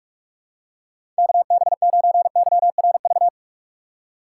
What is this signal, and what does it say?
Morse code, heard as a keyed tone, about 46 wpm, 700 Hz. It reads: KB9YRV
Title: USA Call Signs - 6 Characters - 6x Character Spacing - 45wpm